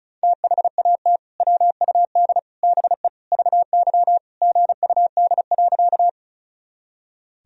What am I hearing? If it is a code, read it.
Morse code, heard as a keyed tone, about 36 wpm, 700 Hz. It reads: THAT WUD BE VY GUD.
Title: Sentences from Top 200 Words and Abbreviations 35wpm